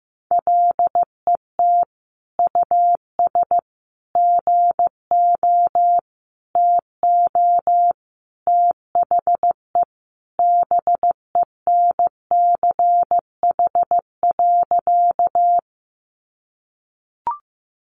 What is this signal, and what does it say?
Morse code, heard as a keyed tone, about 15 wpm, 700 Hz. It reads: LET US GO TO THE BENCH.
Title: Sentences from Top 1400 Words 15wpm